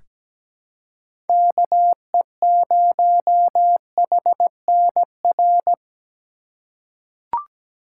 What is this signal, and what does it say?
Morse code, heard as a keyed tone, about 17 wpm, 700 Hz. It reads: KE0HNR
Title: USA Call Signs - 2x Character Spacing - 17wpm